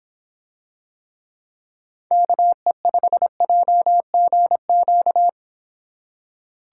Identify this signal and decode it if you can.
Morse code, heard as a keyed tone, about 26 wpm, 700 Hz. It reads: KE5JGQ